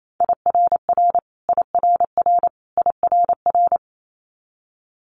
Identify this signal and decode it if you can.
Morse code, heard as a keyed tone, about 28 wpm, 700 Hz. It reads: IRR IRR IRR
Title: Multi-letter Phonemes Repeated 3x 28wpm